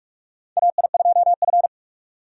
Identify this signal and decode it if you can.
Morse code, heard as a keyed tone, about 46 wpm, 700 Hz. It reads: AI2F
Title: USA Call Signs - 4 Characters - 45wpm